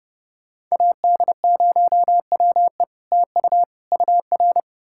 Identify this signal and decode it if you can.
Morse code, heard as a keyed tone, about 40 wpm, 700 Hz. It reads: AD0WE TU UR